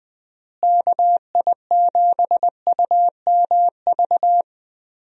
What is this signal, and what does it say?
Morse code, heard as a keyed tone, about 21 wpm, 700 Hz. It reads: KI7UMV
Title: USA Call Signs - 6 Characters - Repeated 3x - 20wpm